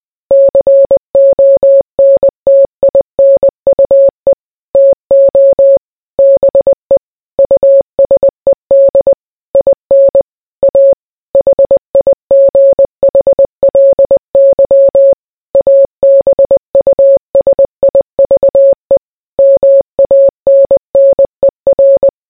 Morse code, heard as a keyed tone, about 20 wpm, 550 Hz. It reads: CONTINUE TO BE USED IN A HIGHLY ABUSIVE MANNER